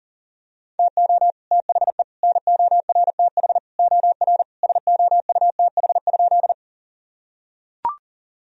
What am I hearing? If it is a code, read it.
Morse code, heard as a keyed tone, about 40 wpm, 700 Hz. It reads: TO THE NORTH OR SOUTH?
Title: Sentences from Top 1000 Words - Review Part 2 - 40wpm